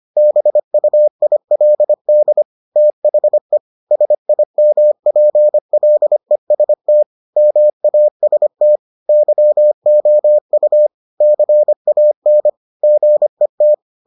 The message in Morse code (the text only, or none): BUILD THE SIMPLEST MAST YOU CAN GET